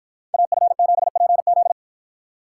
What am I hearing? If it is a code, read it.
Morse code, heard as a keyed tone, about 53 wpm, 700 Hz. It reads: AF7PZ